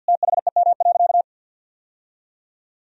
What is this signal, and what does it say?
Morse code, heard as a keyed tone, about 50 wpm, 700 Hz. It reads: THEM.
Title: Sentences from Top 1000 Words - Review Part 4 - 50wpm